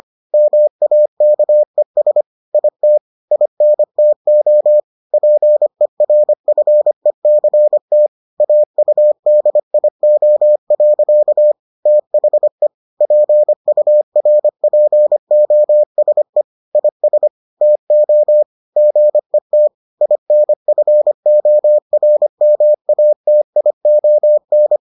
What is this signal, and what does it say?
Morse code, heard as a keyed tone, about 25 wpm, 600 Hz. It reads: MAKES IT INTO PERFECT AUDIO. THE PURPOSE IS TO GET INFORMATION